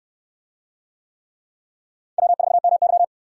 Morse code, heard as a keyed tone, about 68 wpm, 700 Hz. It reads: W4MY